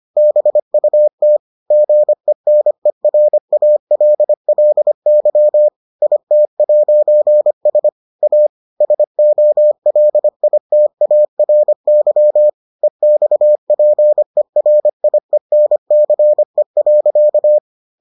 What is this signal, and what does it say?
Morse code, heard as a keyed tone, about 25 wpm, 600 Hz. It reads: BUT GENERALLY IT'S A SOLITARY EXPERIENCE.